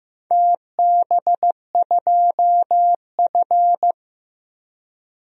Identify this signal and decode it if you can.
Morse code, heard as a keyed tone, about 15 wpm, 700 Hz. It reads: TB2F
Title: DXCC-100 Call Signs - 4 Characters - 15wpm